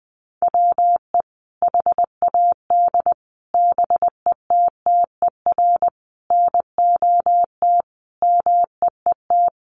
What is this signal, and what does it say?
Morse code, heard as a keyed tone, about 20 wpm, 700 Hz. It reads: WE HAD BETTER NOT MEET
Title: Sentences from Top 1100 Words 20wpm